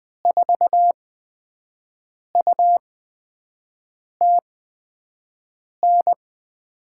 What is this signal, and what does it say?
Morse code, heard as a keyed tone, about 20 wpm, spaced as for 5 wpm, 700 Hz. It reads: 4UTN
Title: USA Call Signs - 6 Characters - 8x Character Spacing - 20wpm